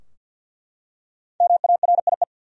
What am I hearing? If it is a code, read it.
Morse code, heard as a keyed tone, about 51 wpm, 700 Hz. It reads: MARIE
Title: USA Common Names - Max 5 Letters - 50wpm